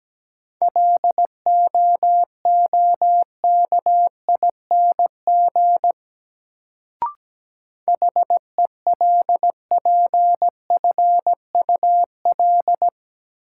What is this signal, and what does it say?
Morse code, heard as a keyed tone, about 17 wpm, 700 Hz. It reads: LOOKING HELPFUL
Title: Sentences from Top 1000 Words - Review Part 6 - New Word Drill - 17wpm